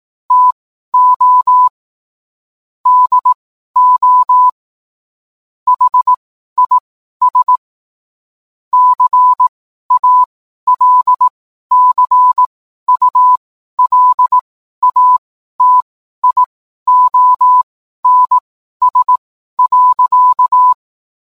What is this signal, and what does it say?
Morse code, heard as a keyed tone, about 18 wpm, spaced as for 12 wpm, 1000 Hz. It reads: TO DO HIS CALCULATIONS.